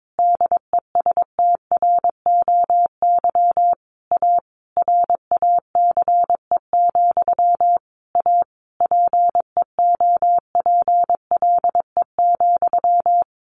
Morse code, heard as a keyed tone, about 22 wpm, 700 Hz. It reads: DESTROY A RACE, A PEOPLE,
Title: A Sound Of Thunder 22wpm